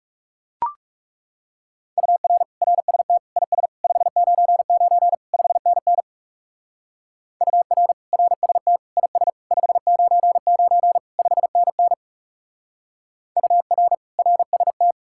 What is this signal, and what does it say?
Morse code, heard as a keyed tone, about 37 wpm, 700 Hz. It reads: UR RST IS 599 5NN UR RST IS 599 5NN UR RST
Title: Speed-Racing Common QSO Phrases Encore - 30wpm